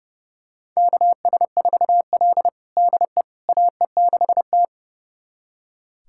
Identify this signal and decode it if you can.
Morse code, heard as a keyed tone, about 32 wpm, 700 Hz. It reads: KS4L DE AE6T
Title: USA Callsign DE Callsign - 4 Characters - 30wpm